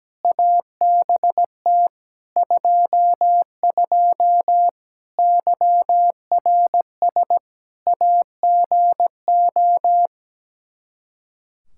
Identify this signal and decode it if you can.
Morse code, heard as a keyed tone, about 17 wpm, 700 Hz. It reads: ABT 22 YRS AGO
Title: Practice QSO - Exchange 3 - More Info - 17wpm